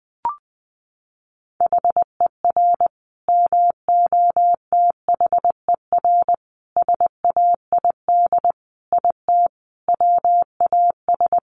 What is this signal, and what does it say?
Morse code, heard as a keyed tone, about 20 wpm, 700 Hz. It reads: HER MOTHER SAID IT WAS
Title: Sentences from Top 1200 Words 20wpm